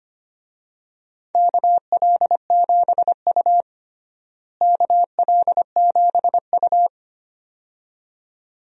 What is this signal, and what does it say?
Morse code, heard as a keyed tone, about 25 wpm, 700 Hz. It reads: KL7U KL7U
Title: DXCC-100 Call Signs - 4 Characters - Repeated 2x - 25wpm